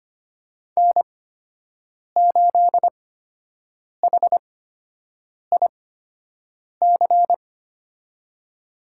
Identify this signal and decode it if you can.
Morse code, heard as a keyed tone, about 25 wpm, spaced as for 7 wpm, 700 Hz. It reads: N8HIC
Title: USA Call Signs - 5 Characters - 8x Character Spacing - 25wpm